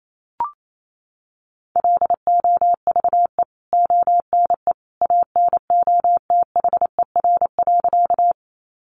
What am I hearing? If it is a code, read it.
Morse code, heard as a keyed tone, about 28 wpm, 700 Hz. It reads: LOVE ONE ANOTHER.
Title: Sentences from Top 600 Words 28wpm